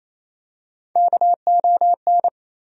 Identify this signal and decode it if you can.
Morse code, heard as a keyed tone, about 28 wpm, 700 Hz. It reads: KON
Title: USA Call Sign Postfixes - 3 Letters 28wpm